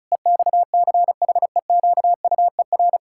Morse code, heard as a keyed tone, about 34 wpm, 700 Hz. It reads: EXCHEQUER
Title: Pangram Challenge v2 - 35wpm